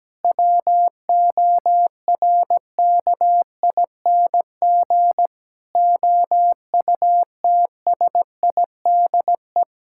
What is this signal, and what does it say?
Morse code, heard as a keyed tone, about 17 wpm, 700 Hz. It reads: WORKING OUTSIDE
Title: Sentences from Top 1000 Words - Review Part 6 - 17wpm